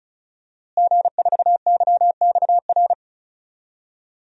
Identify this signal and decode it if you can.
Morse code, heard as a keyed tone, about 34 wpm, 700 Hz. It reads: G4YXR
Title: DXCC-100 Call Signs - 5 Characters - Repeated 2x - 35wpm